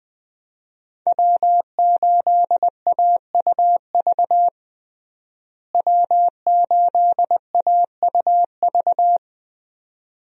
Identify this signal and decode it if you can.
Morse code, heard as a keyed tone, about 20 wpm, 700 Hz. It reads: W8AUV W8AUV
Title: USA Call Signs - 5 Characters - Repeated 2x - 20wpm